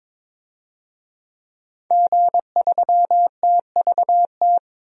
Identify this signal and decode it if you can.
Morse code, heard as a keyed tone, about 22 wpm, 700 Hz. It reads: G3TVT
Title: DXCC-100 Call Signs - 5 Characters - 22wpm